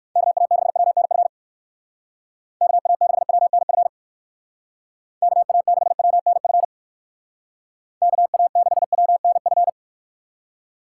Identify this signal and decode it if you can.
Morse code, heard as a keyed tone, about 57 wpm, 700 Hz. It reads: KA6WNF KA6WNF KA6WNF KA6WNF